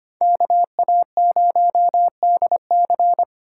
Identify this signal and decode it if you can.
Morse code, heard as a keyed tone, about 25 wpm, 700 Hz. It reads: KA0DC
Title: USA Call Signs - 2x Character Spacing - 25wpm